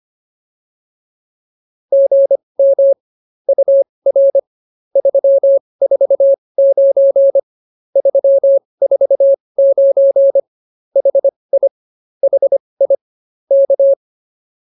Morse code, heard as a keyed tone, about 25 wpm, spaced as for 20 wpm, 590 Hz. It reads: GM UR 349 349 HI HI K